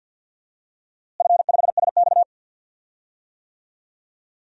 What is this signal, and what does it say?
Morse code, heard as a keyed tone, about 51 wpm, 700 Hz. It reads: F5SX
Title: DXCC-100 Call Signs - 4 Characters - Repeated 3x - 50wpm